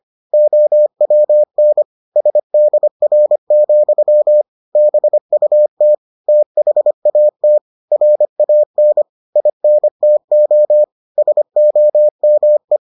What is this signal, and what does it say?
Morse code, heard as a keyed tone, about 25 wpm, 600 Hz. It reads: OWN SDR, BUT THAT RAN INTO SOME